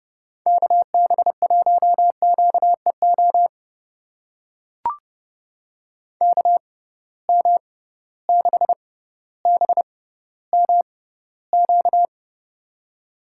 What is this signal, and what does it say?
Morse code, heard as a keyed tone, about 30 wpm, spaced as for 15 wpm, 700 Hz. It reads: KB1QEO KM6BMQ
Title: USA Call Signs - 6 Characters - 6x Character Spacing - 30wpm